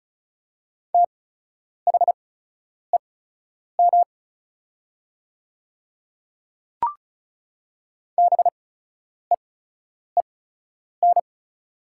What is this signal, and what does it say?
Morse code, heard as a keyed tone, about 36 wpm, spaced as for 10 wpm, 700 Hz. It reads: THEM BEEN
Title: 8x Character Spacing - Top 100 Words - 35wpm